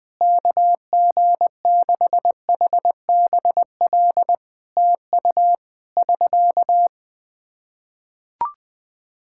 Morse code, heard as a keyed tone, about 21 wpm, 700 Hz. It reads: KG6HBL TU <SK>
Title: K1USN SST QSO Practice 20wpm